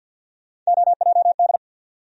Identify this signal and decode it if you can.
Morse code, heard as a keyed tone, about 48 wpm, 700 Hz. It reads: OJD
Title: USA Call Sign Postfixes - 3 Letters 50wpm